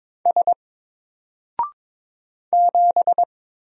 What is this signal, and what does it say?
Morse code, heard as a keyed tone, about 22 wpm, 700 Hz. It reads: S 7
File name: Single-Letter-Number-22wpm